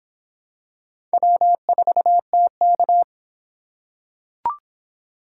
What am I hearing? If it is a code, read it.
Morse code, heard as a keyed tone, about 25 wpm, 700 Hz. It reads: W4TK